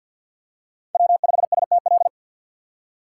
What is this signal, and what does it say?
Morse code, heard as a keyed tone, about 53 wpm, 700 Hz. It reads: W5STL
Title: USA Call Signs - 5 Characters - 8x Character Spacing - 50wpm